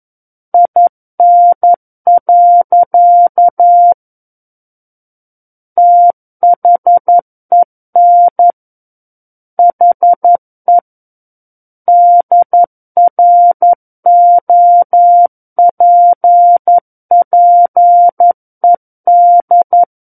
Morse code, heard as a keyed tone, about 11 wpm, 700 Hz. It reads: IN. THEN HE DROPPED
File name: 11 WPM